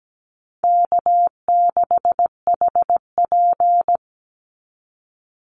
Morse code, heard as a keyed tone, about 17 wpm, 700 Hz. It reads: K6HP
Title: Call Signs Repeated 2x - 17wpm